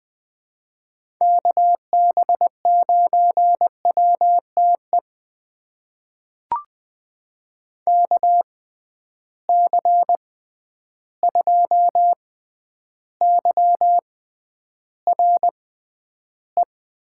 Morse code, heard as a keyed tone, about 20 wpm, spaced as for 10 wpm, 700 Hz. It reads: KB9WTE KC2YRE
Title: USA Call Signs - 6 Characters - 6x Character Spacing - 20wpm